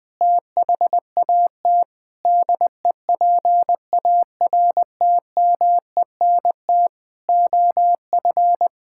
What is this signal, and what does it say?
Morse code, heard as a keyed tone, about 21 wpm, 700 Hz. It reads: THAT DEPARTMENT OF
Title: Sentences from Top 1000 Words - Review Part 5 - 20wpm